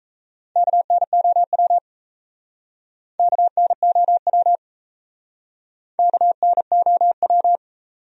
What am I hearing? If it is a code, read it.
Morse code, heard as a keyed tone, about 39 wpm, 700 Hz. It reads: KNOW KNOW KNOW